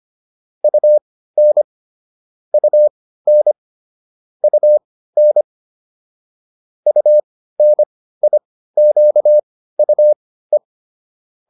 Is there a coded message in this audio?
Yes: Morse code from 0.6 to 10.6 s, heard as a keyed tone, about 25 wpm, spaced as for 15 wpm, 600 Hz. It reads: UN UN UN UNIQUE